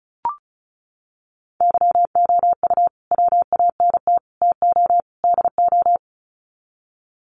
Morse code, heard as a keyed tone, about 34 wpm, 700 Hz. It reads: YOU WANT TO DO
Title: Sets of 4 Words 35wpm